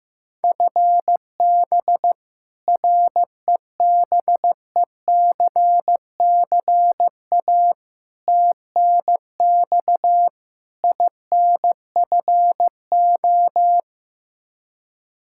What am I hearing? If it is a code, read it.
Morse code, heard as a keyed tone, about 15 wpm, 700 Hz. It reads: FB REBECCA TNX INFO
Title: Practice QSO - Exchange 3 - More Info - 15wpm